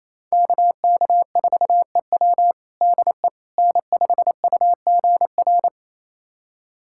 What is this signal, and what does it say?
Morse code, heard as a keyed tone, about 28 wpm, 700 Hz. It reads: KK4EW DE N5UGR
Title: USA Callsign DE Callsign - 5 Characters - 28wpm